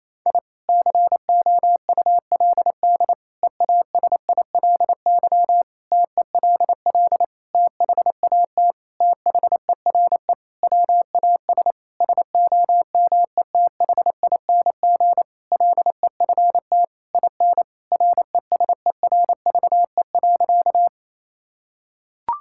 Morse code, heard as a keyed tone, about 27 wpm, 700 Hz. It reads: I COULD EASILY TELL THAT THERE WAS SOMETHING LEFT IN RESERVE.
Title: Sentences from Top 1000 Words - Review Part 6 - 28wpm